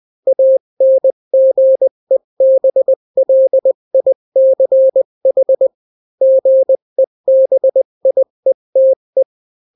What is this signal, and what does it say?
Morse code, heard as a keyed tone, about 20 wpm, 530 Hz. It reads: ANGEBLICH GEBIETE
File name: daily / 18wpm